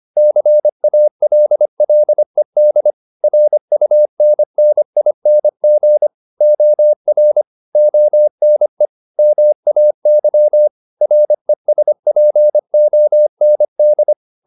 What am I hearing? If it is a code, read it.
Morse code, heard as a keyed tone, about 25 wpm, 600 Hz. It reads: CALLED RUNNING OR ONE MAY RESPOND